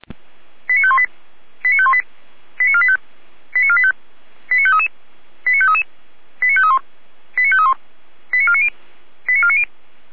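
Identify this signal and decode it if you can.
Five-tone selective call (ZVEI), heard as quick runs of five tones, tones of about 80 ms. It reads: 89518 89518 89485 89485 89533 89533 89531 89531 89490 89490
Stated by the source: ZVEI five-tone selective call received on an SDR.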